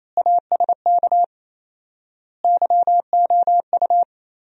Morse code, heard as a keyed tone, about 27 wpm, 700 Hz. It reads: ASK YOU
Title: 4x Word-Spacing - Sentences from Top 200 Words - 28wpm